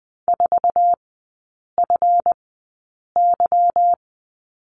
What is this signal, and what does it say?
Morse code, heard as a keyed tone, about 20 wpm, spaced as for 8 wpm, 690 Hz. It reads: 4FY
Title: 145 Lesson 24e - Letter y - Three Characters 20wpm